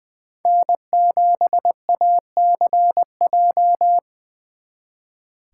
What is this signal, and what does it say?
Morse code, heard as a keyed tone, about 24 wpm, 700 Hz. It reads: N7ACJ
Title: Speed Racing - USA Call Signs 20wpm